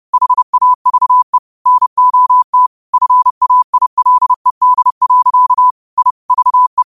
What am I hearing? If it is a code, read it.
Morse code, heard as a keyed tone, about 30 wpm, 1000 Hz. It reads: HAVE NOT FAILED. I VE
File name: qotdfiles / 30wpm